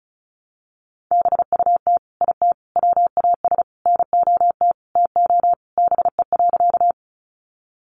Morse code, heard as a keyed tone, about 34 wpm, 700 Hz. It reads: BUT IT WAS NOT TO BE.
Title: Sentences from Top 600 Words 35wpm